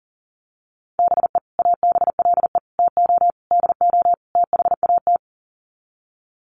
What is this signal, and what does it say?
Morse code, heard as a keyed tone, about 40 wpm, 700 Hz. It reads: BE ABLE TO DO THAT
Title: Sets of 5 Words Encore2 40wpm